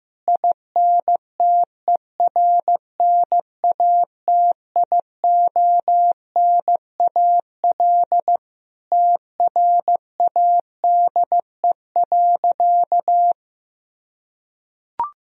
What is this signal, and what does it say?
Morse code, heard as a keyed tone, about 16 wpm, 700 Hz. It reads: INTERNATIONAL TRADE.
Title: Sentences from Top 1000 Words - Review Part 4 - 15wpm